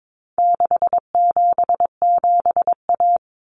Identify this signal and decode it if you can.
Morse code, heard as a keyed tone, about 22 wpm, 700 Hz. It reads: 677A
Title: Common Cut Numbers - 4-Digit 22wpm